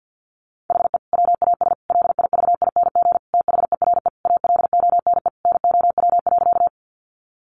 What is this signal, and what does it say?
Morse code, heard as a keyed tone, about 50 wpm, 700 Hz. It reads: HE WAS LIVING THERE ALONE NOW.
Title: Sentences from Top 1100 Words 50wpm